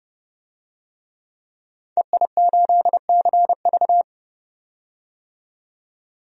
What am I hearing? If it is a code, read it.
Morse code, heard as a keyed tone, about 30 wpm, 700 Hz. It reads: EI8CV
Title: DXCC-100 Call Signs - 5 Characters - 30wpm